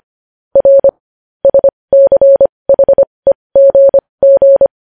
Morse code, heard as a keyed tone, about 25 wpm, spaced as for 20 wpm, 550 Hz. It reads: R SCHEGG
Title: News Headlines at 20